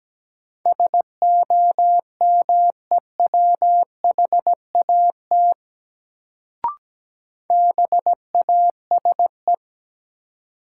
Morse code, heard as a keyed tone, about 17 wpm, 700 Hz. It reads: SOMEWHAT BASE
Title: Sentences from Top 1000 Words - Review Part 1 - New Word Drill - 17wpm